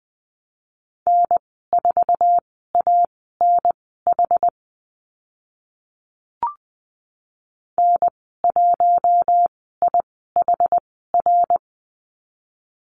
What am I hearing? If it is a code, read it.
Morse code, heard as a keyed tone, about 20 wpm, spaced as for 14 wpm, 700 Hz. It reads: N4ANH N1IHR
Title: 076 Lesson 13g - Letter L - Call Signs 20wpm